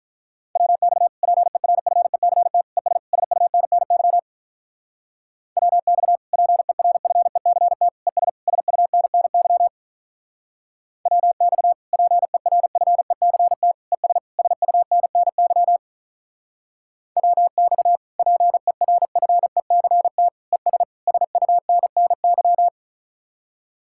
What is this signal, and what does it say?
Morse code, heard as a keyed tone, about 43 wpm, 700 Hz. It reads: WX PERFECT ES SUNNY WX PERFECT ES SUNNY WX PERFECT ES SUNNY WX PERFECT ES SUNNY
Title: Speed-Racing Common QSO Phrases - 35wpm